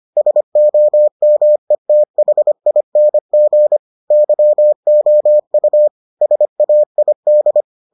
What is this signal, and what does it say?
Morse code, heard as a keyed tone, about 25 wpm, 600 Hz. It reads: SOMETHING YOU SAID